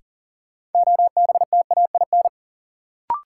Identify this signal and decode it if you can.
Morse code, heard as a keyed tone, about 40 wpm, 700 Hz. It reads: OBTAIN
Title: Sentences from Top 1300 Words - New Word Drill - 40wpm